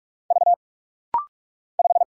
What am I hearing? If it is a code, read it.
Morse code, heard as a keyed tone, about 46 wpm, 700 Hz. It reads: V 5
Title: Single Letter-Number - Rapid-Fire 45wpm